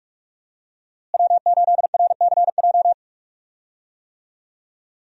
Morse code, heard as a keyed tone, about 46 wpm, 700 Hz. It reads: W8RCJ
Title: USA Call Signs - 5 Characters - Repeated 3x - 45wpm